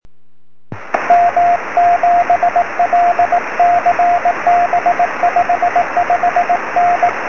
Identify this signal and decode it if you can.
Morse code, heard as a keyed tone, about 18 wpm, 710 Hz, with radio noise. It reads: M7LCB 55N
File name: sm7lcb / 06-ok1kze 23cm